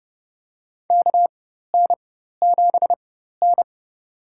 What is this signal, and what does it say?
Morse code, heard as a keyed tone, about 30 wpm, spaced as for 14 wpm, 700 Hz. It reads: KN7N
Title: USA Call Signs - 4x Character Spacing - 30wpm